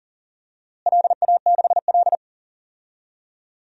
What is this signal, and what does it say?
Morse code, heard as a keyed tone, about 40 wpm, 700 Hz. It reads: LA6L
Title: DXCC-100 Call Signs - 4 Characters - Repeated 3x - 40wpm